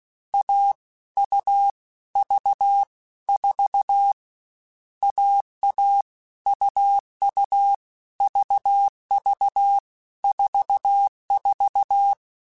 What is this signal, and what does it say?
Morse code, heard as a keyed tone, about 16 wpm, 800 Hz. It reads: A U V 4 AA UU VV 44